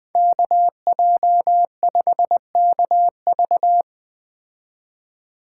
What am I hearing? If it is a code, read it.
Morse code, heard as a keyed tone, about 20 wpm, 700 Hz. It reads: KJ5KV
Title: Speed Racing - USA 2x2 Call Signs 17wpm